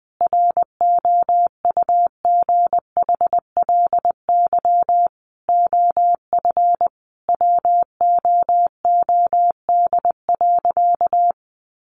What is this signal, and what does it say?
Morse code, heard as a keyed tone, about 20 wpm, 700 Hz. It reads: ROUGHLY OF WOOD.
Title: Sentences from Top 1500 Words 20wpm